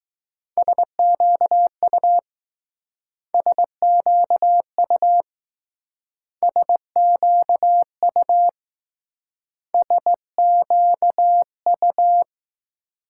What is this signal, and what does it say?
Morse code, heard as a keyed tone, about 20 wpm, 700 Hz. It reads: SQU SQU SQU SQU